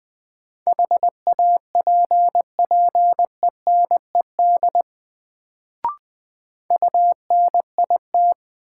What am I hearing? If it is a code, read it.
Morse code, heard as a keyed tone, about 20 wpm, 700 Hz. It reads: HAPPENED UNIT
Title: Sentences from Top 1000 Words - Review Part 2 - New Word Drill - 20wpm